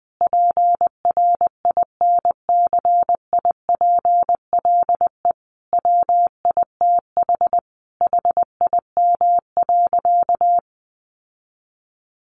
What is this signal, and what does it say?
Morse code, heard as a keyed tone, about 20 wpm, 700 Hz. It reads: PRINCIPLE WITH HIM.
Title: Sentences from Top 1400 Words 20wpm